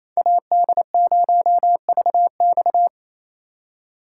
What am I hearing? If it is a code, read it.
Morse code, heard as a keyed tone, about 27 wpm, 700 Hz. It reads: AD0VX
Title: USA Call Signs - 5 Characters - 4x Character Spacing - 28wpm